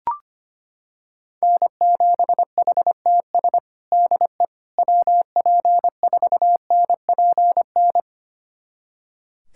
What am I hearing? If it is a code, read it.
Morse code, heard as a keyed tone, about 25 wpm, 700 Hz. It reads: N7HTS DE WP4NPN
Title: USA Callsign DE Callsign 25wpm